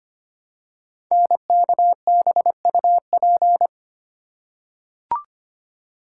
Morse code, heard as a keyed tone, about 25 wpm, 700 Hz. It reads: NKBUP
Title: Code Groups - 5 Letters - 25wpm